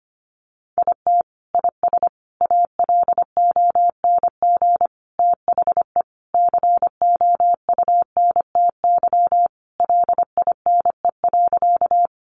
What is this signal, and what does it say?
Morse code, heard as a keyed tone, about 25 wpm, 700 Hz. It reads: IT IS ALONG THE COUNTY LINE.
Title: Sentences from Top 1200 Words 25wpm